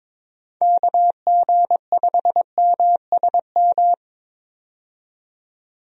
Morse code, heard as a keyed tone, about 22 wpm, 700 Hz. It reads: KG5MSM
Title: USA Call Signs - 6 Characters - 22wpm